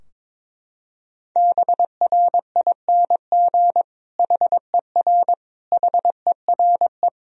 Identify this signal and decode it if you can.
Morse code, heard as a keyed tone, about 22 wpm, 700 Hz. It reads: BRING HER HERE
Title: Sentences from Top 1000 Words - Review Part 3 - 22wpm